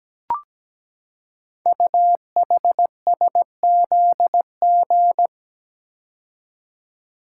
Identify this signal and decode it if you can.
Morse code, heard as a keyed tone, about 17 wpm, 700 Hz. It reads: UHSZG
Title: Code Groups - 5 Letters - 17wpm